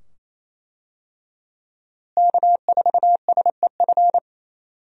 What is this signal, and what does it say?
Morse code, heard as a keyed tone, about 28 wpm, 700 Hz. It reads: K4SEF
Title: USA Call Signs - 5 Characters - Repeated 2x - 28wpm